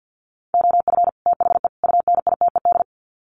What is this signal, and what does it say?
Morse code, heard as a keyed tone, about 50 wpm, 700 Hz. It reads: OF THE UNITED
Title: Sets of 3 Words 50wpm